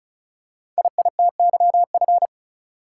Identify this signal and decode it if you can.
Morse code, heard as a keyed tone, about 36 wpm, 700 Hz. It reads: IITYF
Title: Code Groups - 5 Letters - 35wpm